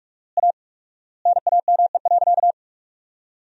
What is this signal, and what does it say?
Morse code, heard as a keyed tone, about 46 wpm, 700 Hz. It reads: A NAME.
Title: 4x Word-Spacing - Sentences from Top 500 Words - 45wpm